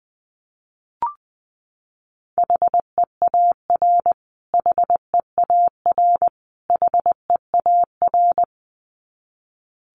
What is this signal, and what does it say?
Morse code, heard as a keyed tone, about 20 wpm, 700 Hz. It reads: HEAR HEAR HEAR
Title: Top 500 Words Repeated 20wpm